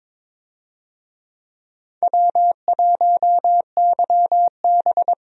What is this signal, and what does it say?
Morse code, heard as a keyed tone, about 22 wpm, 700 Hz. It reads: W1YB
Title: USA Call Signs - 4 Characters - 22wpm